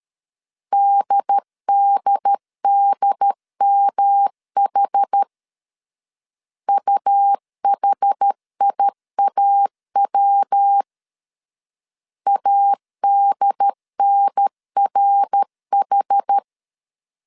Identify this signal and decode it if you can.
Morse code, heard as a keyed tone, about 13 wpm, 800 Hz. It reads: DDDMH UHIAW ADNRH
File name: CW / DET / 1 3 1500C S C 12wpm 2750